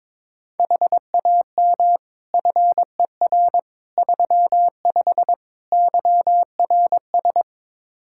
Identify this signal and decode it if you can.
Morse code, heard as a keyed tone, about 21 wpm, 700 Hz. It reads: HAM FER 35 YRS
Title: Practice QSO - Exchange 3 - More Info - 22wpm